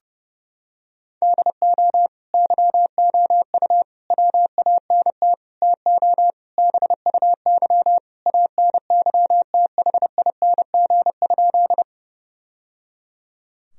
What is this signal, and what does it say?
Morse code, heard as a keyed tone, about 30 wpm, 700 Hz. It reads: DO YOU WANT TO BUY ANYTHING?
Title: Sentences from Top 1000 Words - Review Part 4 - 30wpm